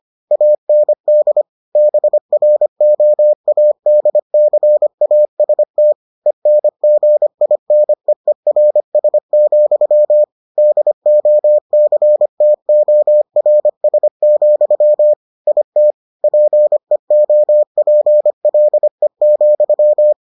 Morse code, heard as a keyed tone, about 25 wpm, 600 Hz. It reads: AND BROADCAST ENGINEERS, DOCTORS, IT PEOPLE,